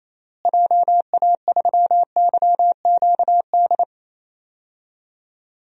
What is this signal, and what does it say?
Morse code, heard as a keyed tone, about 27 wpm, 700 Hz. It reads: JA3YQD
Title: DXCC-100 Call Signs - 6 Characters - Repeated 3x - 28wpm